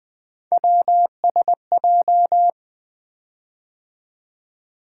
Morse code, heard as a keyed tone, about 20 wpm, 700 Hz. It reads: WSJ
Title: Three Letter-Number Combinations - 20wpm